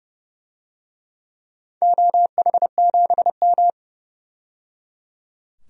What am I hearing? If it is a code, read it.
Morse code, heard as a keyed tone, about 30 wpm, 700 Hz. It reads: OH7M
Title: DXCC-100 Call Signs - 4 Characters - 30wpm